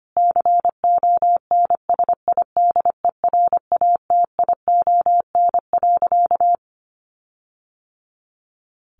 Morse code, heard as a keyed tone, about 25 wpm, 700 Hz. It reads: CONSIDERATION.
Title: Sentences from Top 1500 Words 25wpm